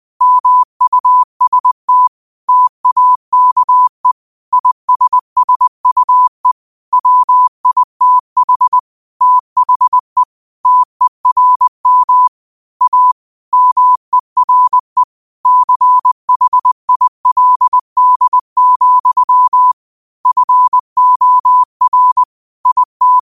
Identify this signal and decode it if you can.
Morse code, heard as a keyed tone, about 20 wpm, 1000 Hz. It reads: MUST TAKE ISSUE WITH THE TERM A MERE CHILD, FOR IT
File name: qotdfiles / 20wpm